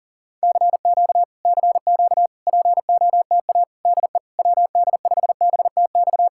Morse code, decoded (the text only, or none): CQ CQ POTA DE WD5BTX